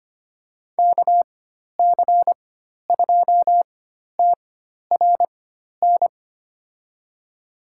Morse code, heard as a keyed tone, about 25 wpm, spaced as for 12 wpm, 700 Hz. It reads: KC2TRN
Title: USA Call Signs - 6 Characters - 4x Character Spacing - 25wpm